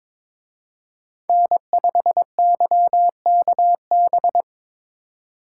Morse code, heard as a keyed tone, about 26 wpm, 700 Hz. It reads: N5YKB
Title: Speed Racing - USA Call Signs 22wpm